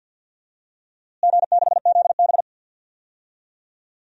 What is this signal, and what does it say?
Morse code, heard as a keyed tone, about 51 wpm, 700 Hz. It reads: G6ZB
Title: DXCC-100 Call Signs - 4 Characters - Repeated 3x - 50wpm